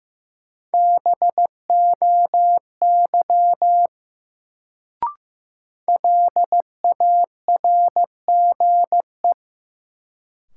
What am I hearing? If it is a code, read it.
Morse code, heard as a keyed tone, about 15 wpm, 700 Hz. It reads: BOY LARGE
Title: Sentences from Top 200 Words - New Word Drill - 15wpm